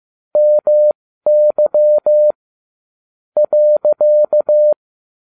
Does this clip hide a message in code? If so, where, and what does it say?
Yes: Morse code from 0.4 to 4.7 s, heard as a keyed tone, about 15 wpm, spaced as for 13 wpm, 600 Hz. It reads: MY .